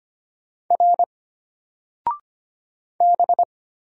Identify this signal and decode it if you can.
Morse code, heard as a keyed tone, about 25 wpm, 700 Hz. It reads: R B
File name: Single-Letter-Number-25wpm